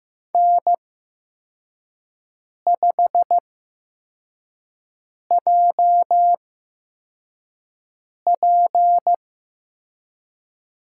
Morse code, heard as a keyed tone, about 15 wpm, spaced as for 4 wpm, 700 Hz. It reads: N5JP
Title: USA Call Signs - 5 Characters - 8x Character Spacing - 15wpm